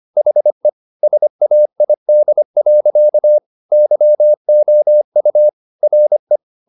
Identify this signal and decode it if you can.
Morse code, heard as a keyed tone, about 25 wpm, 600 Hz. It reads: HE SAID. YOU RE